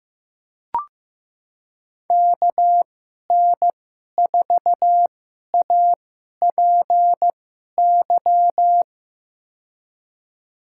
Morse code, heard as a keyed tone, about 15 wpm, spaced as for 11 wpm, 700 Hz. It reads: KN4APY
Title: USA Call Signs - 6 Characters - 2x Character Spacing - 15wpm